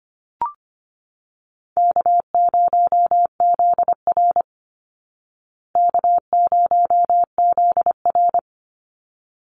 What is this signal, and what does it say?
Morse code, heard as a keyed tone, about 25 wpm, 700 Hz. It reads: K0ZR K0ZR
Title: Easy Call Signs Repeated 2x - 25wpm